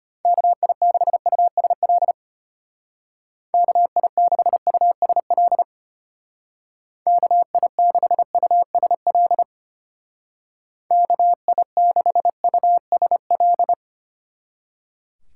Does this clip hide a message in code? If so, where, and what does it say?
Yes: Morse code from 0.3 to 13.7 s, heard as a keyed tone, about 32 wpm, 700 Hz. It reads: KI6USL KI6USL KI6USL KI6USL